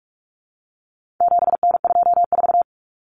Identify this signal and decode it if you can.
Morse code, heard as a keyed tone, about 44 wpm, 700 Hz. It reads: 7N24
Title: Common Cut Numbers - 4-Digit 45wpm